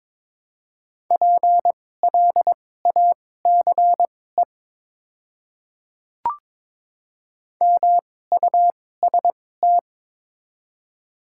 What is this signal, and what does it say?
Morse code, heard as a keyed tone, about 22 wpm, spaced as for 16 wpm, 700 Hz. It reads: PLACE MUST
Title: 2x Character Spacing - Top 200 Words - 22wpm